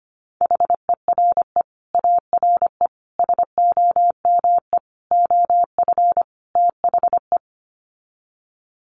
Morse code, heard as a keyed tone, about 25 wpm, 700 Hz. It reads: HERE ARE SOME OF THE
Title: Sets of 5 Words Encore 25wpm